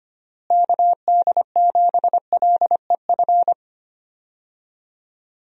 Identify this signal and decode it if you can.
Morse code, heard as a keyed tone, about 25 wpm, 700 Hz. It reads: KD7LEF
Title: USA Call Signs - 6 Characters - Repeated 2x - 25wpm